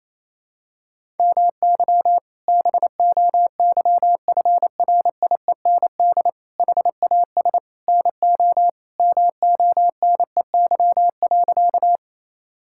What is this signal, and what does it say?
Morse code, heard as a keyed tone, about 28 wpm, 700 Hz. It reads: MY BOYFRIEND HAS NO MONEY.
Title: Sentences from Top 1000 Words - Review Part 5 - 28wpm